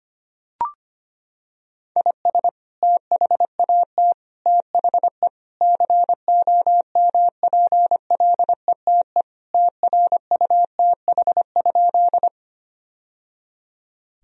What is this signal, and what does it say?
Morse code, heard as a keyed tone, about 25 wpm, 700 Hz. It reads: IS THAT THE COMPLETE TRUTH?
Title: Sentences from Top 1000 Words - Review Part 2 - 25wpm